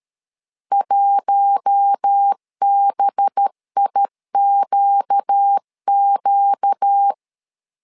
Morse code, heard as a keyed tone, about 13 wpm, 800 Hz. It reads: 1BIQQ
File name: LEZIONI / 5 LBVC / 5 LB 12wpm 1906